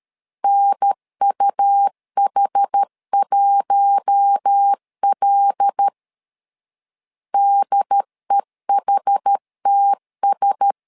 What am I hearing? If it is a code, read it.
Morse code, heard as a keyed tone, about 13 wpm, 800 Hz. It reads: NUH1L DEHTS
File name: LEZIONI / 5 LBVC / 5 L 12wpm 1913